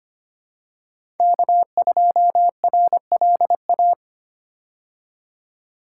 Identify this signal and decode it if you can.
Morse code, heard as a keyed tone, about 25 wpm, 700 Hz. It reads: K2RLA
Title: Code Groups - 5 Letter:Numbers - 25wpm